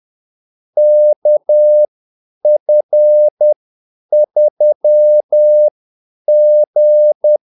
Morse code, heard as a keyed tone, about 10 wpm, spaced as for 8 wpm, 600 Hz. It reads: KF3G
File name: Callsigns-10-8wpm